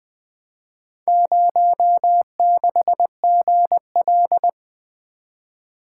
Morse code, heard as a keyed tone, about 20 wpm, 700 Hz. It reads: 06GL
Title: Four Letter-Number Combinations - 20wpm